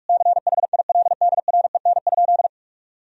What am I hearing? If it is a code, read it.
Morse code, heard as a keyed tone, about 41 wpm, 700 Hz. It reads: CHILDREN?
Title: Sentences from Top 1000 Words - Review Part 2 - 45wpm